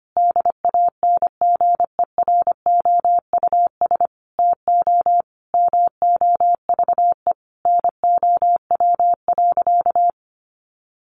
Morse code, heard as a keyed tone, about 25 wpm, 700 Hz. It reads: DANGEROUS TO MOVE NOW.
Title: Sentences from Top 1200 Words 25wpm